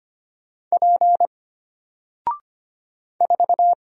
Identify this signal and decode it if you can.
Morse code, heard as a keyed tone, about 25 wpm, 700 Hz. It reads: P 4
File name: Single-Letter-Number-25wpm